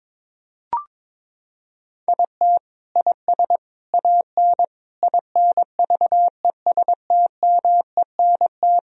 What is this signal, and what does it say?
Morse code, heard as a keyed tone, about 22 wpm, 700 Hz. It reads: IT IS AN INVESTMENT
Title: Sentences from Top 1000 Words - Review Part 4 - 22wpm